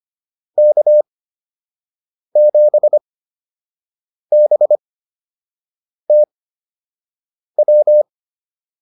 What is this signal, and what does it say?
Morse code, heard as a keyed tone, about 25 wpm, spaced as for 6 wpm, 600 Hz. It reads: K7BTW
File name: QSO 203 06